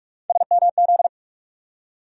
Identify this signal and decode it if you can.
Morse code, heard as a keyed tone, about 45 wpm, 700 Hz. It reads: SMZ
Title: USA Call Sign Postfixes - 3 Letters 45wpm